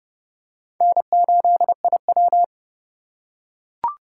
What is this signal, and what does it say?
Morse code, heard as a keyed tone, about 30 wpm, 700 Hz. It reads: N8IW
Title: USA Call Signs - 4 Characters - 3x Character Spacing - 30wpm